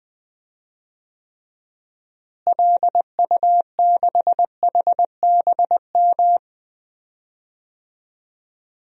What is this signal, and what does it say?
Morse code, heard as a keyed tone, about 20 wpm, 700 Hz. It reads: LU6HBM
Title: DXCC-100 Call Signs - 6 Characters - Repeated 2x - 20wpm